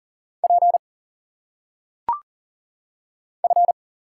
Morse code, heard as a keyed tone, about 40 wpm, 700 Hz. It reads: P F
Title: Single Letters - Phonetic Pronunciation 40wpm